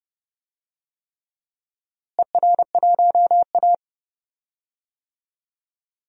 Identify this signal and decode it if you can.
Morse code, heard as a keyed tone, about 30 wpm, 700 Hz. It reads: ER1A
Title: DXCC-100 Call Signs - 4 Characters - Repeated 3x - 30wpm